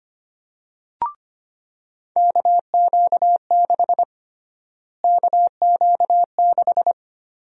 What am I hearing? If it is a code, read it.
Morse code, heard as a keyed tone, about 25 wpm, 700 Hz. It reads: KQ6 KQ6
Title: USA Call Sign Prefixes - Repeated 2x - 25wpm